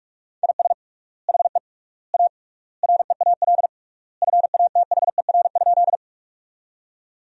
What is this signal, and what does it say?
Morse code, heard as a keyed tone, about 46 wpm, 700 Hz. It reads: IS HE A REAL FATHER?
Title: 3x Word-Spacing - Sentences from Top 300 Words - 45wpm